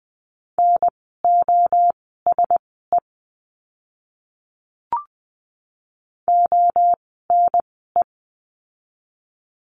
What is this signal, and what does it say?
Morse code, heard as a keyed tone, about 20 wpm, spaced as for 15 wpm, 700 Hz. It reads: NOSE ONE
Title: 054 Lesson 10f - Letter r - Words 20wpm